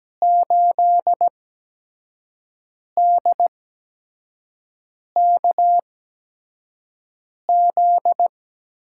Morse code, heard as a keyed tone, about 17 wpm, spaced as for 5 wpm, 700 Hz. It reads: 8DKZ
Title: USA Call Signs - 6 Characters - 8x Character Spacing - 17wpm